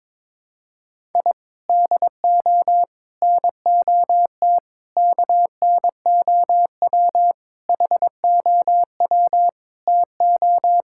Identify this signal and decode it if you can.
Morse code, heard as a keyed tone, about 22 wpm, 700 Hz. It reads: I DO NOT KNOW HOW TO
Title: Sentences from Top 1000 Words - Review Part 6 - 22wpm